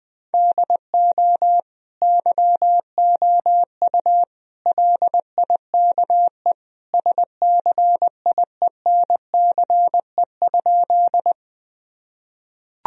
Morse code, heard as a keyed tone, about 20 wpm, 700 Hz. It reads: DO YOU LIKE SCIENCE?
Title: Sentences from Top 1000 Words - Review Part 2 - 20wpm